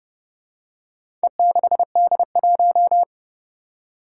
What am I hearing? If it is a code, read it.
Morse code, heard as a keyed tone, about 28 wpm, 700 Hz. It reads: E6D1
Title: Four Challenging Letter-Number Combinations 30wpm